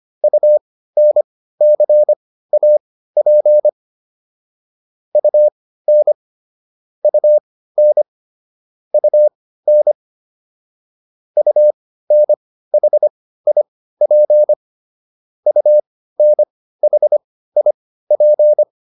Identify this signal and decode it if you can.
Morse code, heard as a keyed tone, about 25 wpm, spaced as for 15 wpm, 600 Hz. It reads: UNCAP UN UN UN UNHIP UNHIP